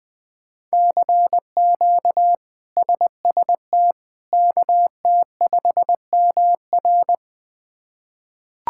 Morse code, heard as a keyed tone, about 21 wpm, 700 Hz. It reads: CQ SST KT5MR
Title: K1USN SST QSO Practice - 20wpm